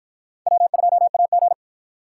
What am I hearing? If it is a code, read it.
Morse code, heard as a keyed tone, about 53 wpm, 700 Hz. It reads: W2AG